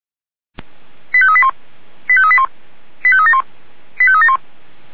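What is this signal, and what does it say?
Five-tone selective call (ZVEI), heard as quick runs of five tones, tones of about 70 ms. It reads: 85381 85381 85371 85371
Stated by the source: ZVEI five-tone selective call received by SDR.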